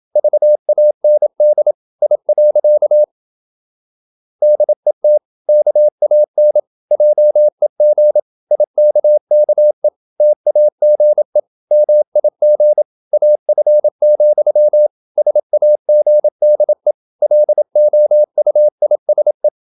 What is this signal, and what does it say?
Morse code, heard as a keyed tone, about 27 wpm, 600 Hz. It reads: VAND I. DET KAN JEG IKKE TAGE MIG AF, SAGDE LOUISE
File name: Ferie QRQ 27wpm